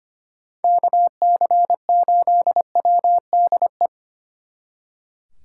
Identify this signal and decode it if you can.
Morse code, heard as a keyed tone, about 30 wpm, 700 Hz. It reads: KC8WDE
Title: Speed Racing - USA 2x3 Call Signs 25wpm